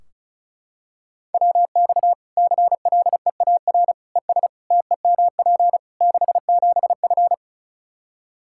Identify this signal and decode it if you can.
Morse code, heard as a keyed tone, about 34 wpm, 700 Hz. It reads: WX CLEAR ES TEMP 67F
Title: K1USN SST QSO Practice 35wpm